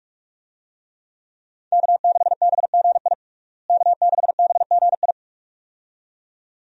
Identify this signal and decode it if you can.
Morse code, heard as a keyed tone, about 46 wpm, 700 Hz. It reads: K6BGI K6BGI
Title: USA Call Signs - 5 Characters - Repeated 2x - 45wpm